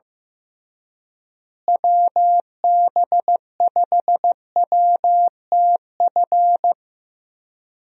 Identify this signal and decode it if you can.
Morse code, heard as a keyed tone, about 15 wpm, 700 Hz. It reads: WB5WTF
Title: USA Call Signs - 6 Characters - Repeated 2x - 15wpm